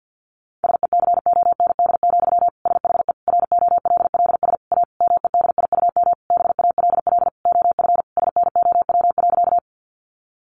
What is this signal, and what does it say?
Morse code, heard as a keyed tone, about 50 wpm, 700 Hz. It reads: SECOND, SHE ROLLS A MEDIUM BALL OF SNOW.
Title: Easy Phrases 50wpm